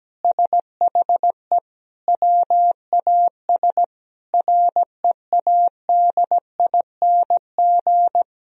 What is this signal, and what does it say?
Morse code, heard as a keyed tone, about 17 wpm, 700 Hz. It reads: SHE WAS READING
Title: Sentences from Top 1000 Words - Review Part 3 - 17wpm